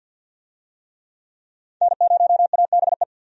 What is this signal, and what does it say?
Morse code, heard as a keyed tone, about 52 wpm, 700 Hz. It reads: N0ABE
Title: USA Call Signs - 5 Characters - 50wpm